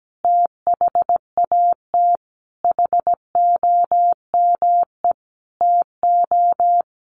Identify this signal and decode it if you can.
Morse code, heard as a keyed tone, about 17 wpm, 700 Hz. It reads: THAT HOME TO
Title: Sentences from Top 1100 Words 17wpm